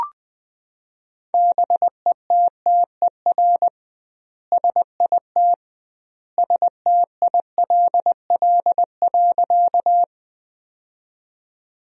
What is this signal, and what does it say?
Morse code, heard as a keyed tone, about 20 wpm, 700 Hz. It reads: BETTER SIT STILL.
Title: 2x Word-Spacing - Sentences from Top 500 Words - 20wpm